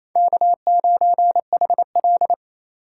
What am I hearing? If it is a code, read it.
Morse code, heard as a keyed tone, about 27 wpm, 700 Hz. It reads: K9HL
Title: USA Call Signs - 4 Characters - 3x Character Spacing - 28wpm